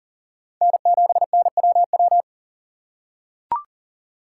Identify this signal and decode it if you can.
Morse code, heard as a keyed tone, about 37 wpm, 700 Hz. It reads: N7NWW
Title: USA Call Signs - 5 Characters - 2x Character Spacing - 40wpm